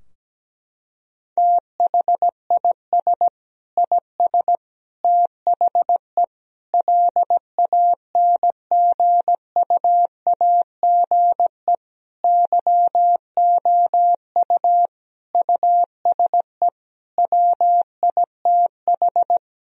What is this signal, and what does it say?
Morse code, heard as a keyed tone, about 17 wpm, 700 Hz. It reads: THIS IS THE LANGUAGE YOU USE WITH
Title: Sentences from Top 1000 Words - Review Part 3 - 17wpm